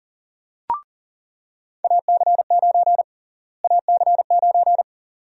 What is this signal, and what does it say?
Morse code, heard as a keyed tone, about 40 wpm, 700 Hz. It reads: AC9 AC9
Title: USA Call Sign Prefixes - Repeated 2x - 40wpm